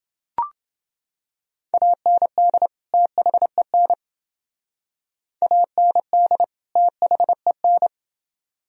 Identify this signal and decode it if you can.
Morse code, heard as a keyed tone, about 29 wpm, 700 Hz. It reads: AND THEN AND THEN